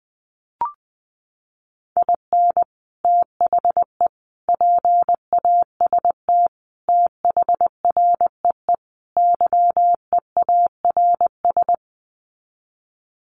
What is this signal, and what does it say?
Morse code, heard as a keyed tone, about 20 wpm, 700 Hz. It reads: IN THE PAST THREE YEARS
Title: Sets of 5 Words Encore2 20wpm